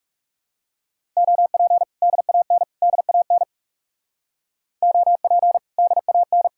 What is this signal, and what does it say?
Morse code, heard as a keyed tone, about 36 wpm, 700 Hz. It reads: OP DAN DAN OP DAN
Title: Speed-Racing Common QSO Phrases Encore - 30wpm